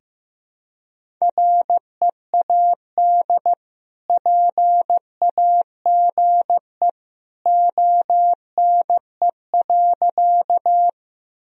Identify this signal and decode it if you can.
Morse code, heard as a keyed tone, about 15 wpm, 700 Hz. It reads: READ PAGE ONE.
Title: Sentences from Top 1000 Words - Review Part 1 - 15wpm